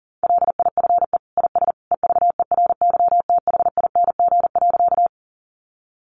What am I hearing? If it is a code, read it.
Morse code, heard as a keyed tone, about 40 wpm, 700 Hz. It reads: LIFE IS EVERYTHING.
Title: Sentences from Top 700 Words 40wpm